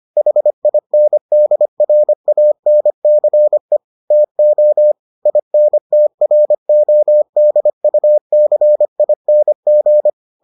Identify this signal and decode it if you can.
Morse code, heard as a keyed tone, about 25 wpm, 600 Hz. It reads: HINDRANCE TO INTRODUCING